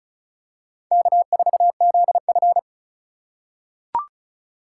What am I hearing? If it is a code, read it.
Morse code, heard as a keyed tone, about 36 wpm, 700 Hz. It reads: K4ZF
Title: USA Call Signs - 4 Characters - 4x Character Spacing - 35wpm